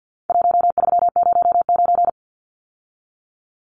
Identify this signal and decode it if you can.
Morse code, heard as a keyed tone, about 50 wpm, 700 Hz. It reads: 1309
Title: Numbers - 4-Digit 50wpm